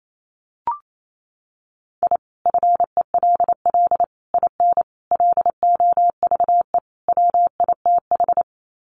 Morse code, heard as a keyed tone, about 28 wpm, 700 Hz. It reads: I FELL IN LOVE WITH
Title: Sets of 5 Words Encore 28wpm